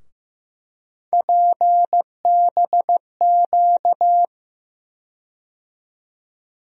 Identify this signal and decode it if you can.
Morse code, heard as a keyed tone, about 15 wpm, 700 Hz. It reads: PBQ
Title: USA Call Sign Postfixes - 3 Letters 15wpm